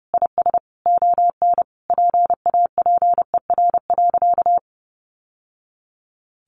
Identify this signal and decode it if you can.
Morse code, heard as a keyed tone, about 30 wpm, 700 Hz. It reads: IS ON PAPER.
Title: Sentences from Top 1200 Words 30wpm